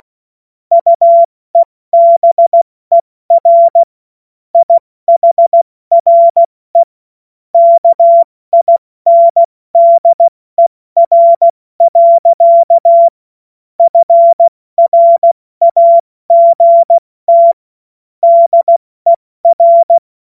Morse code, heard as a keyed tone, about 16 wpm, 680 Hz. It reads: UEBER IHRE KINDER. FRAGT DER
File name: Morsekurs / Witze / Witz 16 April 2024 70 F80